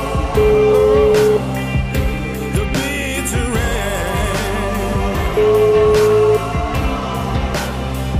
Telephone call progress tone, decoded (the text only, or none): ringback tone